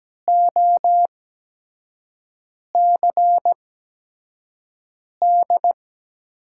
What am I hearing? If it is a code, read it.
Morse code, heard as a keyed tone, about 17 wpm, spaced as for 5 wpm, 700 Hz. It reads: OCD
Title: USA Call Signs - 6 Characters - 8x Character Spacing - 17wpm